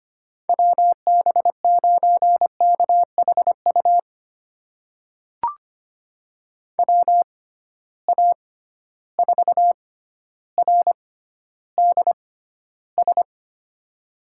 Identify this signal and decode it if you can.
Morse code, heard as a keyed tone, about 25 wpm, spaced as for 13 wpm, 700 Hz. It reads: WB9KHU WA4RDS
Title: USA Call Signs - 6 Characters - 6x Character Spacing - 25wpm